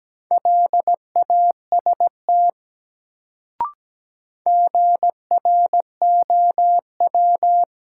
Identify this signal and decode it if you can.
Morse code, heard as a keyed tone, about 17 wpm, 700 Hz. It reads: LAST GROW
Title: Sentences from Top 300 Words - New Word Drill - 17wpm